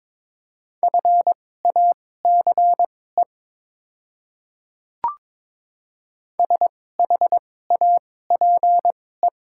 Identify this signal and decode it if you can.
Morse code, heard as a keyed tone, about 22 wpm, spaced as for 16 wpm, 700 Hz. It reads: FACE SHAPE
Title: 2x Character Spacing - Top 500 Words - 22wpm